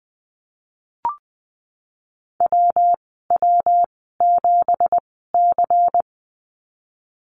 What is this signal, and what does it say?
Morse code, heard as a keyed tone, about 20 wpm, spaced as for 15 wpm, 700 Hz. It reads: WW7C
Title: 207 Lesson 33g - Letter j - Call Signs 20wpm